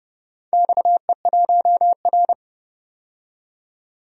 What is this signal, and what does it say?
Morse code, heard as a keyed tone, about 29 wpm, 700 Hz. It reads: XE1R
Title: DXCC-100 Call Signs - 4 Characters - 30wpm